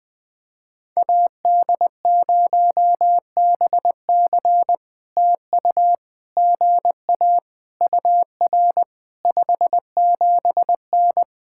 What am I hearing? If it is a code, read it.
Morse code, heard as a keyed tone, about 20 wpm, 700 Hz. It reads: AD0BC TU GA UR 57N
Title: POTA QSO Practice - 20wpm